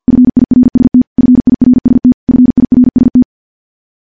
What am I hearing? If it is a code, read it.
Morse code, heard as a keyed tone, about 50 wpm, 250 Hz. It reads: FIRST FIRST FIRST
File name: morse / morse-training-50WPM